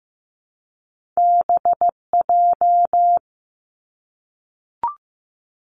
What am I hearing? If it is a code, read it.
Morse code, heard as a keyed tone, about 15 wpm, 700 Hz. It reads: BJ
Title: Top Two Letter Combinations in Words 15wpm